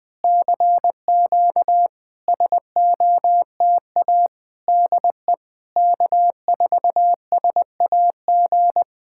Morse code, heard as a keyed tone, about 19 wpm, 700 Hz. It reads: CQ SOTA DE K4SAG
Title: SOTA QSO Practice 20wpm